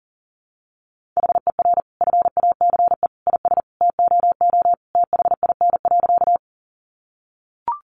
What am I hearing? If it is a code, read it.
Morse code, heard as a keyed tone, about 40 wpm, 700 Hz. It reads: HER FACE IS TOO THIN.
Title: Sentences from Top 1200 Words 40wpm